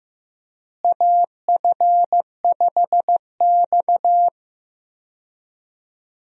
Morse code, heard as a keyed tone, about 15 wpm, 700 Hz. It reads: AF5X
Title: USA Call Signs - 4 Characters - Repeated 3x - 15wpm